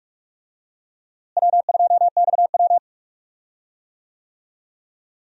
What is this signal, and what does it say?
Morse code, heard as a keyed tone, about 44 wpm, 700 Hz. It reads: W2XW
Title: USA Call Signs - 4 Characters - Repeated 3x - 45wpm